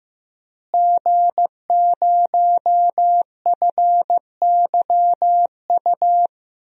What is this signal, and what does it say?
Morse code, heard as a keyed tone, about 15 wpm, 700 Hz. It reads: G0FYU
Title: DXCC-100 Call Signs - 5 Characters - Repeated 2x - 15wpm